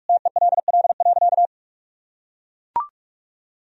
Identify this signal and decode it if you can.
Morse code, heard as a keyed tone, about 41 wpm, 700 Hz. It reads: TELL.
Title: Sentences from Top 1000 Words - Review Part 4 - 45wpm